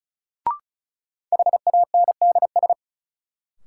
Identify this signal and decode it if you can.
Morse code, heard as a keyed tone, about 36 wpm, 700 Hz. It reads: HANDS
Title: Sets of 4 Words - New Word Drill - 35wpm